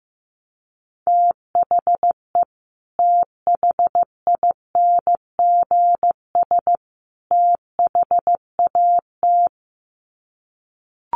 Morse code, heard as a keyed tone, about 15 wpm, 700 Hz. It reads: THE THINGS THAT
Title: Sets of 3 Words 15wpm